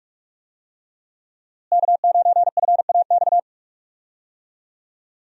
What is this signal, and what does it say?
Morse code, heard as a keyed tone, about 45 wpm, 700 Hz. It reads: K9FAX
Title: USA Call Signs - 5 Characters - Repeated 3x - 45wpm